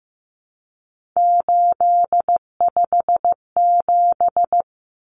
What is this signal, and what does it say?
Morse code, heard as a keyed tone, about 15 wpm, 700 Hz. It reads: 857
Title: Common Cut Numbers - 3-Digit 15wpm